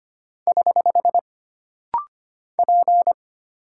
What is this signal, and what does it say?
Morse code, heard as a keyed tone, about 25 wpm, 700 Hz. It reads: <HH> P
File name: Single Character (Letter:Number:Punctuation:Prosign) - Rapid-Fire - 25wpm